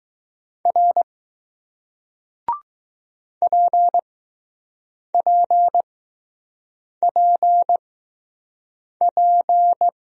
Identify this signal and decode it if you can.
Morse code, heard as a keyed tone, about 20 wpm, spaced as for 6 wpm, 700 Hz. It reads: R PPPP